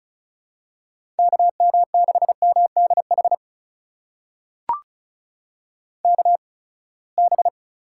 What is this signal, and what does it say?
Morse code, heard as a keyed tone, about 36 wpm, spaced as for 24 wpm, 700 Hz. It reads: KM6MDH KB
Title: USA Call Signs - 8x Character Spacing - 35wpm